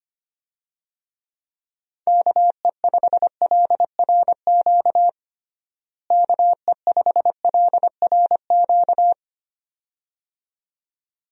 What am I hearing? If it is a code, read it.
Morse code, heard as a keyed tone, about 25 wpm, 700 Hz. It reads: KE5LRQ KE5LRQ
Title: USA Call Signs - 6 Characters - Repeated 2x - 25wpm